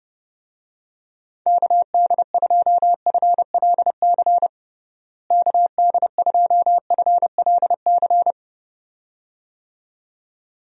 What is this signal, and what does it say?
Morse code, heard as a keyed tone, about 32 wpm, 700 Hz. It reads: KD2FLC KD2FLC
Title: USA Call Signs - 6 Characters - Repeated 2x - 30wpm